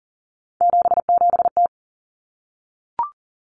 Morse code, heard as a keyed tone, about 40 wpm, 700 Hz. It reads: 77T
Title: Common Cut Numbers - 3-Digit 40wpm